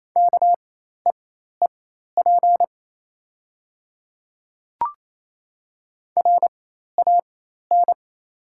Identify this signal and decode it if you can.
Morse code, heard as a keyed tone, about 28 wpm, spaced as for 13 wpm, 700 Hz. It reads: KEEP RAN
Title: 4x Character Spacing - Top 500 Words - 28wpm